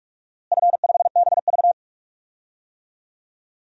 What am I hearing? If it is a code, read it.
Morse code, heard as a keyed tone, about 46 wpm, 700 Hz. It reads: F5BV
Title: DXCC-100 Call Signs - 4 Characters - Repeated 2x - 45wpm